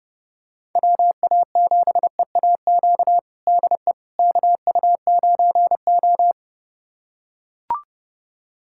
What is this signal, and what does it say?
Morse code, heard as a keyed tone, about 30 wpm, 700 Hz. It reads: WA7EAQ DE KU9O
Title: USA Callsign DE Callsign 30wpm